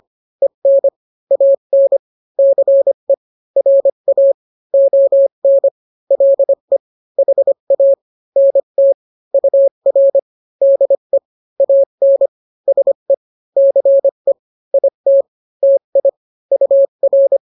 Morse code, heard as a keyed tone, about 25 wpm, 550 Hz. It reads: EN AN CE RA ON LE HA NT UR DE AN SE CE IT TI UR